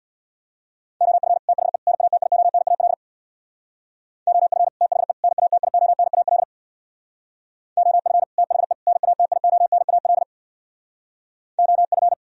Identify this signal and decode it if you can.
Morse code, heard as a keyed tone, about 64 wpm, 700 Hz. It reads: OF THE NATIONAL OF THE NATIONAL OF THE NATIONAL OF